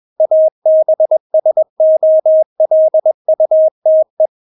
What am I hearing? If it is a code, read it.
Morse code, heard as a keyed tone, about 21 wpm, 630 Hz. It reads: ABSOLUTE
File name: Witz2 21wpm